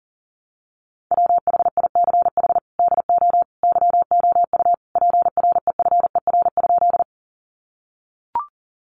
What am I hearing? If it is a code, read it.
Morse code, heard as a keyed tone, about 40 wpm, 700 Hz. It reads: WHICH DO YOU PREFER?
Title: Sentences from Top 1100 Words 40wpm